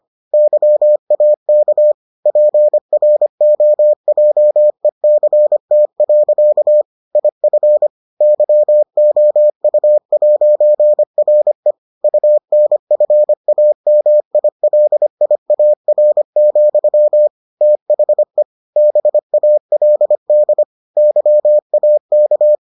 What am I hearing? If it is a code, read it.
Morse code, heard as a keyed tone, about 25 wpm, 600 Hz. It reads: YAK PROJECT. IF YOU'RE UNFAMILIAR, THE BALD YAK